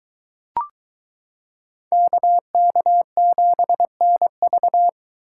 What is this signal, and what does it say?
Morse code, heard as a keyed tone, about 22 wpm, 700 Hz. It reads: KK7NV